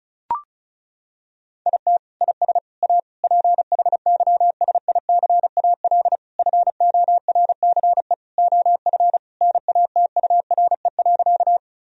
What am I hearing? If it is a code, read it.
Morse code, heard as a keyed tone, about 36 wpm, 700 Hz. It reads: IT IS A PHYSICAL FORCE OF NATURE.
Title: Sentences from Top 1000 Words - Review Part 4 - 35wpm